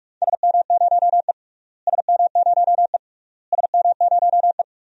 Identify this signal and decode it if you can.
Morse code, heard as a keyed tone, about 46 wpm, 700 Hz. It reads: SM0E SM0E SM0E
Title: DXCC-100 Call Signs - 4 Characters - Repeated 3x - 45wpm